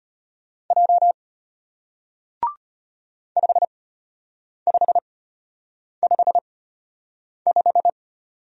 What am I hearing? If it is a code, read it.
Morse code, heard as a keyed tone, about 33 wpm, spaced as for 8 wpm, 700 Hz. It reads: J 5555